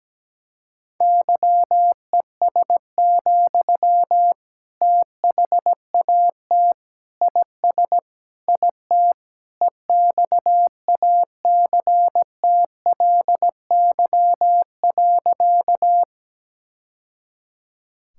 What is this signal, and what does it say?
Morse code, heard as a keyed tone, about 17 wpm, 700 Hz. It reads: YES, THAT IS IT EXACTLY.
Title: Sentences from Top 1000 Words - Review Part 6 - 17wpm